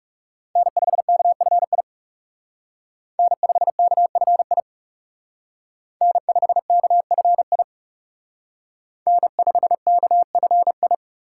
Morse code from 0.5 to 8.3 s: N5KFI N5KFI N5KFI

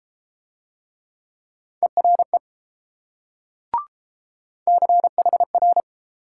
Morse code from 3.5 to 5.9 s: CHR

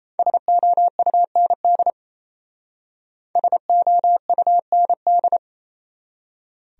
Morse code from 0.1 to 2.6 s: SOUND

